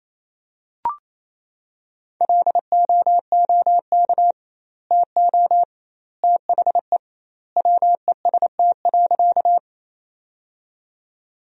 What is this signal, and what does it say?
Morse code, heard as a keyed tone, about 28 wpm, 700 Hz. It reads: LOOK TO THE WEST.
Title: 2x Word-Spacing - Sentences from Top 400 Words - 28wpm